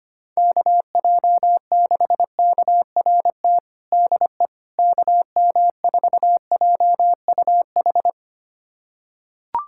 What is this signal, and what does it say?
Morse code, heard as a keyed tone, about 25 wpm, 700 Hz. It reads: KJ6KRT DE KM4JUH
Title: USA Callsign DE Callsign 25wpm